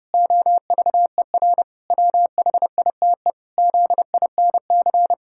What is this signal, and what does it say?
Morse code, heard as a keyed tone, about 30 wpm, 700 Hz. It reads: OVER WHITE ZINC
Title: Pangram Challenge v2 - 30wpm